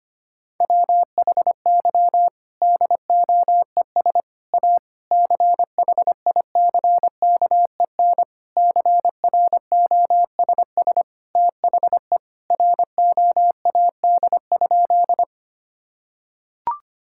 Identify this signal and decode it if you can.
Morse code, heard as a keyed tone, about 25 wpm, 700 Hz. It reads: WHY DOES A CHICKEN CROSS THE ROAD?
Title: Sentences from Top 1000 Words - Review Part 5 - 25wpm